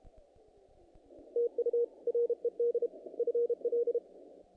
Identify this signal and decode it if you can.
Morse code, heard as a keyed tone, about 32 wpm, 470 Hz. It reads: TU RED FL
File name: contests / naqp-cw-k5tr-op-k5pi- -2000